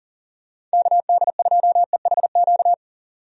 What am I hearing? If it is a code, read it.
Morse code, heard as a keyed tone, about 40 wpm, 700 Hz. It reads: KD2EHQ
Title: USA Call Signs - 6 Characters - 40wpm